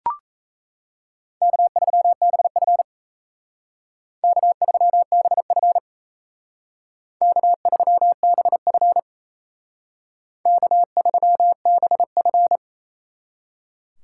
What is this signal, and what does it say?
Morse code, heard as a keyed tone, about 35 wpm, 700 Hz. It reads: K3BF K3BF K3BF K3BF